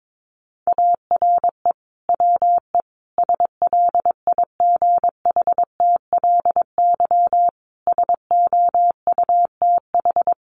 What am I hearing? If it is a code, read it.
Morse code, heard as a keyed tone, about 22 wpm, 700 Hz. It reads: ARE WE SLIGHTLY SOUTH
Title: Sentences from Top 1300 Words 22wpm